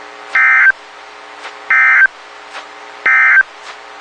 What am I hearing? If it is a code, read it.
Emergency Alert System (SAME) tones, heard as harsh buzzing bursts. It is heard from 0.2 to 3.5 s.
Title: WXK91 NOAA Weather Radio